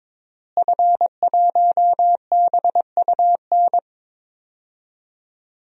Morse code, heard as a keyed tone, about 22 wpm, 700 Hz. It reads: F1BUN
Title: DXCC-100 Call Signs - 5 Characters - Repeated 3x - 22wpm